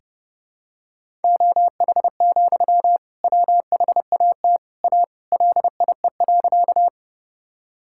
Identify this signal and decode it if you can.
Morse code, heard as a keyed tone, about 30 wpm, 700 Hz. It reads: OH, WHAT A LIE.
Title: Sentences from Top 1600 Words 30wpm